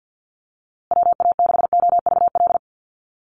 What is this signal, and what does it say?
Morse code, heard as a keyed tone, about 50 wpm, 700 Hz. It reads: WA6OVL
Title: USA Call Signs Repeated 2x 50wpm